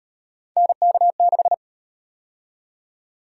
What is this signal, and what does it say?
Morse code, heard as a keyed tone, about 37 wpm, 700 Hz. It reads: NK6